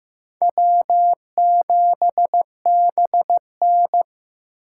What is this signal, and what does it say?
Morse code, heard as a keyed tone, about 15 wpm, 700 Hz. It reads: W7BN
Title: USA Call Signs - 4 Characters - 3x Character Spacing - 15wpm